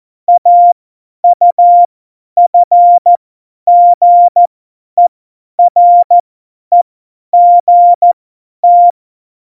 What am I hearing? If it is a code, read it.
Morse code, heard as a keyed tone, about 14 wpm, spaced as for 10 wpm, 700 Hz. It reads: AUFGEREGT
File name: Morsekurs / Witze / Witz 05 Mai 2020, 50 F70